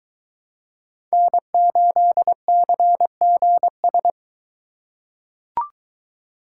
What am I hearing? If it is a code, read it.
Morse code, heard as a keyed tone, about 21 wpm, 700 Hz. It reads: N8CGS